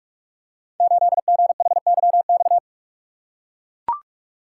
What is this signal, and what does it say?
Morse code, heard as a keyed tone, about 44 wpm, 700 Hz. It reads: 8GHYX
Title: Code Groups - 5 Letter:Numbers - 45wpm